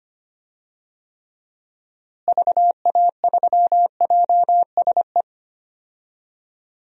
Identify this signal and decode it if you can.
Morse code, heard as a keyed tone, about 25 wpm, 700 Hz. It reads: VA3JSE
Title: DXCC-100 Call Signs - 6 Characters - Repeated 3x - 25wpm